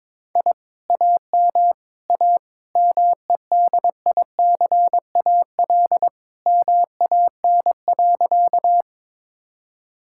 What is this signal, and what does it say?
Morse code, heard as a keyed tone, about 22 wpm, 700 Hz. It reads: I AM A MEDICAL MAN.
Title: Sentences from Top 1000 Words - Review Part 5 - 22wpm